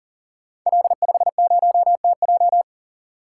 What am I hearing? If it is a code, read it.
Morse code, heard as a keyed tone, about 40 wpm, 690 Hz. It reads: L50TJ
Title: Code Groups - 5 Letter:Numbers - 40wpm